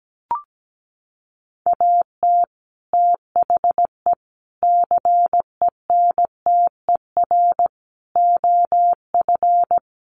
Morse code, heard as a keyed tone, about 17 wpm, 700 Hz. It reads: AT THE CENTER OF
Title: Sets of 5 Words Encore2 17wpm